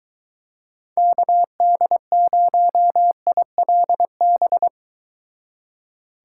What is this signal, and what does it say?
Morse code, heard as a keyed tone, about 23 wpm, 700 Hz. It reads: KD0ILB